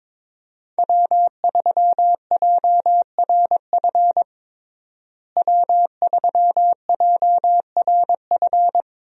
Morse code, heard as a keyed tone, about 22 wpm, 700 Hz. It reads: W3JRF W3JRF
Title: USA Call Signs - 5 Characters - Repeated 2x - 22wpm